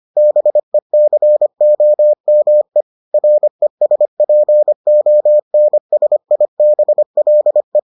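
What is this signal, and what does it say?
Morse code, heard as a keyed tone, about 25 wpm, 600 Hz. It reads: BECOME RESPONSIBLE
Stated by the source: Morse edition of the Foundations of Amateur Radio episodes.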